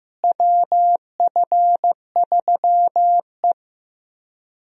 Morse code, heard as a keyed tone, about 15 wpm, 700 Hz. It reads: WF3E
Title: USA Call Signs - 4 Characters - Repeated 2x - 15wpm